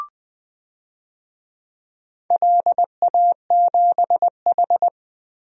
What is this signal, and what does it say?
Morse code, heard as a keyed tone, about 20 wpm, 700 Hz. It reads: LA7H
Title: DXCC-100 Call Signs - 4 Characters - Repeated 3x - 20wpm